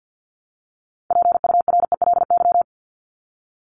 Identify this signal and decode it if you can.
Morse code, heard as a keyed tone, about 50 wpm, 700 Hz. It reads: PURELY
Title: LY Suffix - Top 100 Words with Base Words 50wpm